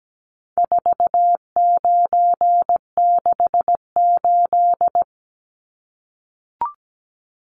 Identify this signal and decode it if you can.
Morse code, heard as a keyed tone, about 17 wpm, 700 Hz. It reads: 4968
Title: Numbers - 4-Digit 17wpm